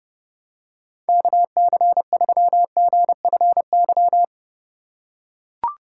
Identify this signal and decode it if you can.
Morse code, heard as a keyed tone, about 30 wpm, 700 Hz. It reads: KC3GFY